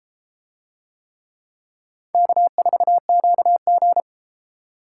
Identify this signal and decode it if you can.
Morse code, heard as a keyed tone, about 32 wpm, 700 Hz. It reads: K4QG